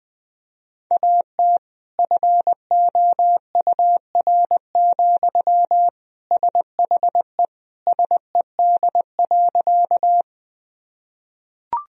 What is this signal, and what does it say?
Morse code, heard as a keyed tone, about 21 wpm, 700 Hz. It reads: AT FOUR, SHE SED.
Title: Sentences from Top 300 Words and Abbreviations 20wpm